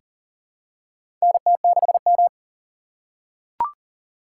Morse code, heard as a keyed tone, about 38 wpm, 700 Hz. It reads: NT6M
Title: USA Call Signs - 4 Characters - 3x Character Spacing - 40wpm